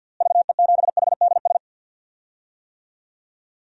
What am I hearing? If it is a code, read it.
Morse code, heard as a keyed tone, about 52 wpm, 700 Hz. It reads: VE7HDS
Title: DXCC-100 Call Signs - 6 Characters - Repeated 2x - 50wpm